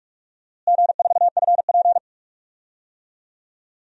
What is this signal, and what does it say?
Morse code, heard as a keyed tone, about 41 wpm, 700 Hz. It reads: G4FP
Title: DXCC-100 Call Signs - 4 Characters - Repeated 3x - 45wpm